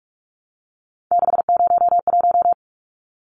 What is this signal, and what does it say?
Morse code, heard as a keyed tone, about 44 wpm, 700 Hz. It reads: 601
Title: Numbers - 3-Digit 45wpm